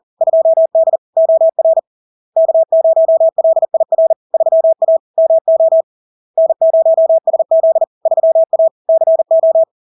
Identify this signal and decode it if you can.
Morse code, heard as a keyed tone, about 40 wpm, 650 Hz. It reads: 2D OR K0LIR 3A MO N0SZ 3A CO
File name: LARK-FD-2022-Log-in-CW-40wpm-650-Hz